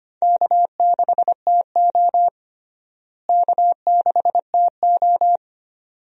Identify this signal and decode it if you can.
Morse code, heard as a keyed tone, about 25 wpm, 700 Hz. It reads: K6TO K6TO
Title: USA Call Signs - 4 Characters - Repeated 2x - 25wpm